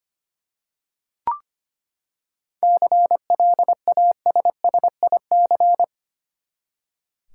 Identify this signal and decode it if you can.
Morse code, heard as a keyed tone, about 25 wpm, 700 Hz. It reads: CLASSIC
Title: Seven Letter Words - 25wpm